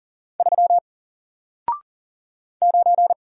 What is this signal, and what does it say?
Morse code, heard as a keyed tone, about 40 wpm, 700 Hz. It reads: 3 9
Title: Single Character (Letter:Number:Punctuation:Prosign) Alt 40wpm